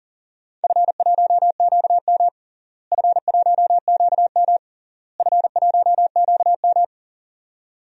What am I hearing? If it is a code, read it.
Morse code, heard as a keyed tone, about 40 wpm, 700 Hz. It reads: F1QM F1QM F1QM
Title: DXCC-100 Call Signs - 4 Characters - Repeated 3x - 40wpm